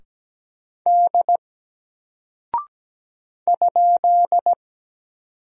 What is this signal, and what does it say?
Morse code, heard as a keyed tone, about 17 wpm, 700 Hz. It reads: D ?
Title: Single Character (Letter:Number:Punctuation:Prosign) Alt 17wpm